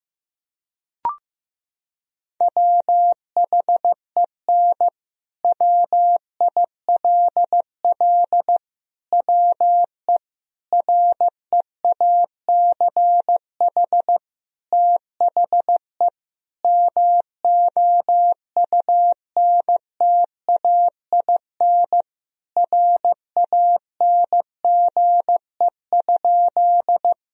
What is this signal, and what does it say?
Morse code, heard as a keyed tone, about 16 wpm, 700 Hz. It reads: WHEN WILL WE REACH THE MOUNTAIN RANGE?
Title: Sentences from Top 1000 Words - Review Part 4 - 15wpm